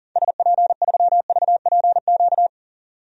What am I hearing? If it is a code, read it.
Morse code, heard as a keyed tone, about 38 wpm, 700 Hz. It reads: SP3VPQ
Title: DXCC-100 Call Signs - 6 Characters - 40wpm